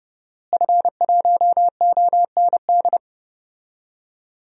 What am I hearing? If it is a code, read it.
Morse code, heard as a keyed tone, about 30 wpm, 700 Hz. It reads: F1OND
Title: DXCC-100 Call Signs - 5 Characters - Repeated 2x - 30wpm